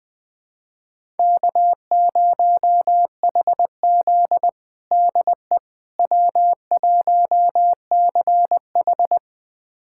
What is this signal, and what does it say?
Morse code, heard as a keyed tone, about 20 wpm, 700 Hz. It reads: K0HZ DE W1CH
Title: USA Callsign DE Callsign - 4 Characters - 20wpm